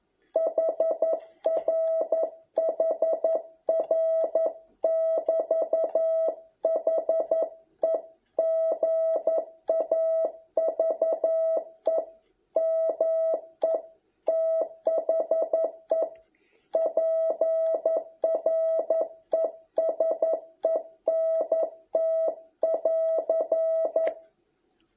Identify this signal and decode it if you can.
Morse code, heard as a keyed tone, about 11 wpm, 650 Hz. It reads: HR HR <BT> HE GAVE ME THE PRESENT <AR>